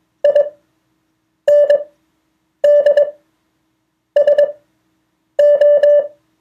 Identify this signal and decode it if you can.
Morse code, heard as a keyed tone, about 21 wpm, spaced as for 7 wpm, 580 Hz. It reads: INDSO